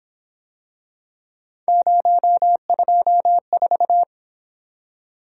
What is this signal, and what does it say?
Morse code, heard as a keyed tone, about 27 wpm, 700 Hz. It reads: 024
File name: Speed-Racing - Three Letter-Number Combinations - 22wpm